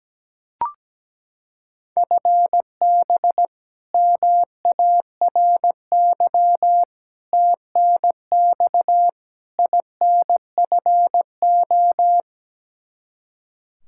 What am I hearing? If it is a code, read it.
Morse code, heard as a keyed tone, about 17 wpm, 700 Hz. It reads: FB MARY TNX INFO
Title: Practice QSO - Exchange 3 - More Info - 17wpm